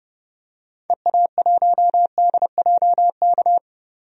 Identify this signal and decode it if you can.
Morse code, heard as a keyed tone, about 30 wpm, 700 Hz. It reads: EA1DJK
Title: DXCC-100 Call Signs - 6 Characters - 30wpm